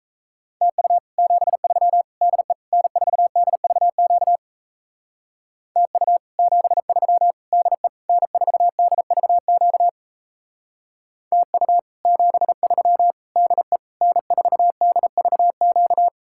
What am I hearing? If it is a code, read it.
Morse code, heard as a keyed tone, about 36 wpm, 700 Hz. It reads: TU 73 DE N4DVQ TU 73 DE N4DVQ TU 73 DE N4DVQ